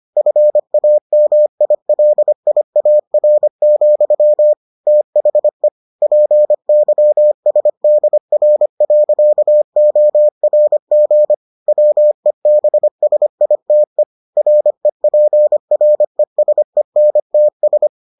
Morse code, heard as a keyed tone, about 25 wpm, 600 Hz. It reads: FAMILIAR, THE PYSDR.ORG WEBSITE REPRESENTS